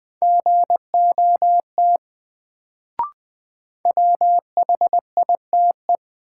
Morse code, heard as a keyed tone, about 20 wpm, 700 Hz. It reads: GOT WHITE
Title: Sets of 2 Words - New Word Drill - 20wpm